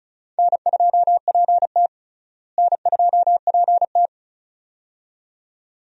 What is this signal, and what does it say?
Morse code, heard as a keyed tone, about 36 wpm, 700 Hz. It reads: N2PT N2PT
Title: USA Call Signs - 4 Characters - Repeated 2x - 35wpm